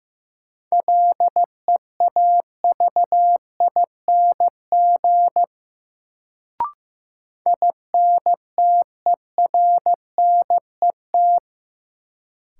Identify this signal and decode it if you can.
Morse code, heard as a keyed tone, about 15 wpm, 700 Hz. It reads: LEAVING INTERNET
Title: Sentences from Top 1000 Words - Review Part 4 - New Word Drill - 15wpm